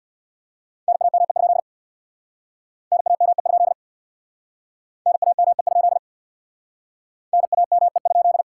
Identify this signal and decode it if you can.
Morse code, heard as a keyed tone, about 62 wpm, 700 Hz. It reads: NAME? NAME? NAME? NAME?
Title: Speed-Racing Common QSO Phrases - 50wpm